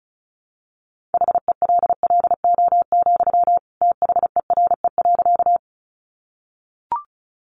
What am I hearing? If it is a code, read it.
Morse code, heard as a keyed tone, about 34 wpm, 700 Hz. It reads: HELLO, THERE.
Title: Sentences from Top 1100 Words 35wpm